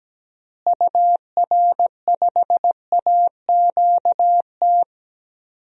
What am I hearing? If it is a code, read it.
Morse code, heard as a keyed tone, about 17 wpm, 700 Hz. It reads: UR5AQT
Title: DXCC-100 Call Signs - 6 Characters - Repeated 2x - 17wpm